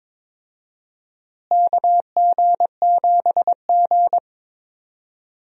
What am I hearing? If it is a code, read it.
Morse code, heard as a keyed tone, about 21 wpm, 700 Hz. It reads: KG7G
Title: USA Call Signs - 4 Characters - Repeated 2x - 22wpm